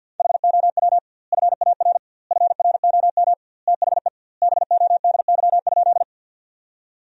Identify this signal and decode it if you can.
Morse code, heard as a keyed tone, about 50 wpm, 700 Hz. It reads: HOW FAR FROM THE BODY?
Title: 2x Word-Spacing - Sentences from Top 400 Words - 50wpm